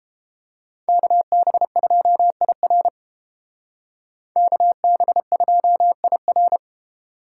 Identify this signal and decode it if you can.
Morse code, heard as a keyed tone, about 29 wpm, 700 Hz. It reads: KB2IR KB2IR